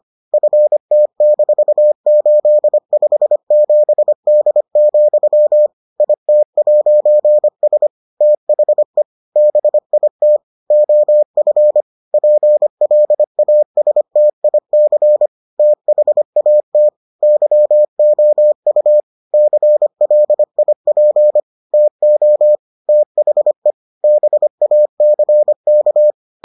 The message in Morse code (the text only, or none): FT-857D, IT'S THE BIT OF PLASTIC THAT YOU CLIP TO THE BACK